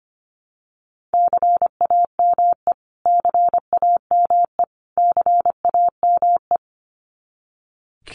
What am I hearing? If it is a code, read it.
Morse code, heard as a keyed tone, about 25 wpm, 700 Hz. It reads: CAME CAME CAME
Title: Top 300 Words Repeated 25wpm